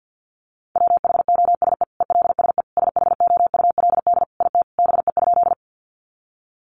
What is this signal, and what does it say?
Morse code, heard as a keyed tone, about 50 wpm, 700 Hz. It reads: WHOSE ELSE SHOULD IT BE?
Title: Sentences from Top 1200 Words 50wpm